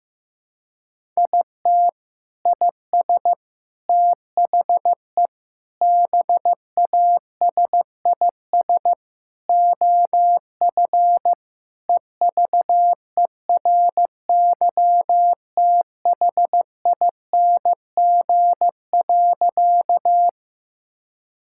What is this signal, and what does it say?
Morse code, heard as a keyed tone, about 15 wpm, 700 Hz. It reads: IT IS THE BASIS OF EVERYTHING.
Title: Sentences from Top 1000 Words - Review Part 5 - 15wpm